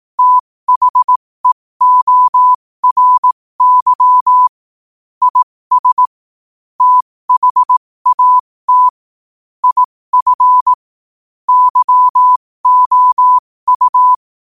Morse code, heard as a keyed tone, about 18 wpm, spaced as for 15 wpm, 1000 Hz. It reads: THEORY IS THAT IF YOU